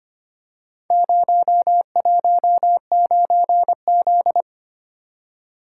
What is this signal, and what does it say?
Morse code, heard as a keyed tone, about 25 wpm, 700 Hz. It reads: 019Z
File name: practice / 25wpm / Four Letter-Number Combinations - 25wpm